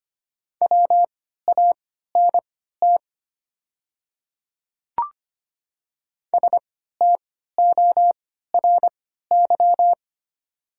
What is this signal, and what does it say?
Morse code, heard as a keyed tone, about 25 wpm, spaced as for 14 wpm, 700 Hz. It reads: WANT STORY
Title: 3x Character Spacing - Top 300 Words - 25wpm